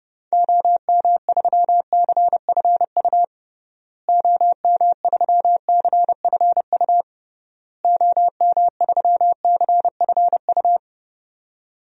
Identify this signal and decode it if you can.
Morse code, heard as a keyed tone, about 32 wpm, 700 Hz. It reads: OM3CFU OM3CFU OM3CFU
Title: DXCC-100 Call Signs - 6 Characters - Repeated 3x - 30wpm